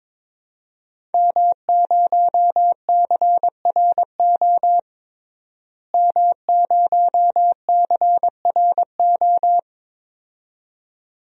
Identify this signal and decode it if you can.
Morse code, heard as a keyed tone, about 22 wpm, 700 Hz. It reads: M0CRO M0CRO
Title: DXCC-100 Call Signs - 5 Characters - Repeated 2x - 22wpm